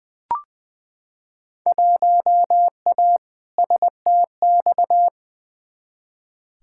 Morse code, heard as a keyed tone, about 20 wpm, 700 Hz. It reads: 1A STX
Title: Field Day Exchanges - 20wpm